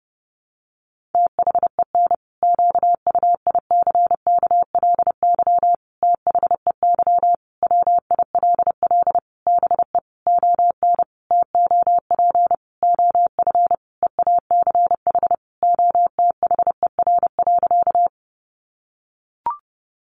Morse code, heard as a keyed tone, about 30 wpm, 700 Hz. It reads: THEN QUICKLY THEY WILL BE ON TOP OF EACH OTHER.
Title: Sentences from Top 700 Words 30wpm